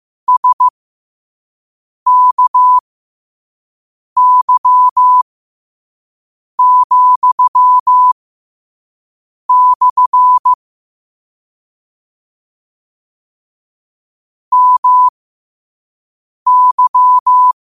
Morse code, heard as a keyed tone, about 15 wpm, spaced as for 5 wpm, 1000 Hz. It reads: SKY,/ MY
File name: qotdfiles / 5wpm